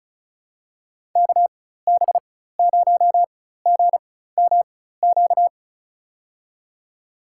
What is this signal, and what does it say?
Morse code, heard as a keyed tone, about 34 wpm, spaced as for 16 wpm, 700 Hz. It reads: KB0GMQ
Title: USA Call Signs - 4x Character Spacing - 35wpm